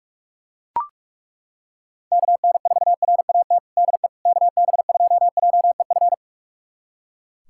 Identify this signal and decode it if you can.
Morse code, heard as a keyed tone, about 41 wpm, 700 Hz. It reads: KN4RAT DE KB2JEF
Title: USA Callsign DE Callsign - 6 Characters - 45wpm